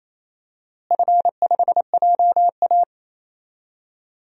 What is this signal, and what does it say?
Morse code, heard as a keyed tone, about 28 wpm, 700 Hz. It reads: F5JA
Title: DXCC-100 Call Signs - 4 Characters - Repeated 3x - 28wpm